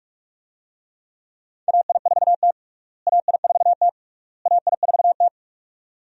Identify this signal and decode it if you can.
Morse code, heard as a keyed tone, about 46 wpm, 700 Hz. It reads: AI4T AI4T AI4T
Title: USA Call Signs - 4 Characters - Repeated 3x - 45wpm